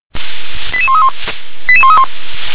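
Five-tone selective call (ZVEI), heard as quick runs of five tones, tones of about 70 ms. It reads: 88131 88131
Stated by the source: ZVEI five-tone call 88083 received by SDR.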